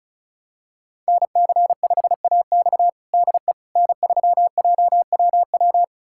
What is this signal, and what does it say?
Morse code, heard as a keyed tone, about 36 wpm, 700 Hz. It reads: NC5AX DE N3JWW
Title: USA Callsign DE Callsign - 5 Characters - 35wpm